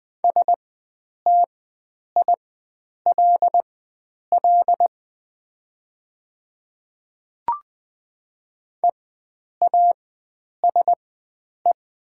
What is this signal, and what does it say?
Morse code, heard as a keyed tone, about 20 wpm, spaced as for 9 wpm, 700 Hz. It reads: STILL EASE
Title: 4x Character Spacing - Top 300 Words - 20wpm